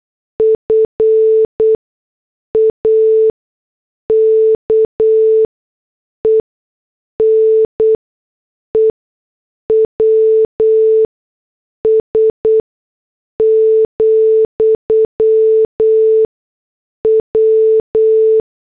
Morse code, heard as a keyed tone, about 8 wpm, spaced as for 6 wpm, 440 Hz. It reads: FAKE NEWS, W
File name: morse / headlines